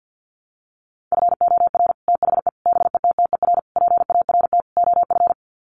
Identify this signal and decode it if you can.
Morse code, heard as a keyed tone, about 50 wpm, 700 Hz. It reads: FOR THE BETTER PART OF
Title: Sets of 5 Words Encore2 50wpm